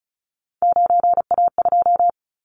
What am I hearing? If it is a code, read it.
Morse code, heard as a keyed tone, about 36 wpm, 690 Hz. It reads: 9A2
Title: Callsign Prefixes for Top 50 DXCC - 35wpm